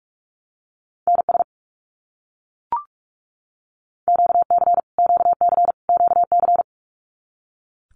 Morse code, heard as a keyed tone, about 44 wpm, 700 Hz. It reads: NS QC QC QC
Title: Canadian Provinces and Territories - Repeated 3x - 45wpm